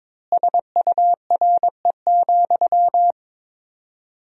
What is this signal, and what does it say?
Morse code, heard as a keyed tone, about 22 wpm, 700 Hz. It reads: SURE,
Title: 4x Word-Spacing - Sentences from Top 500 Words - 22wpm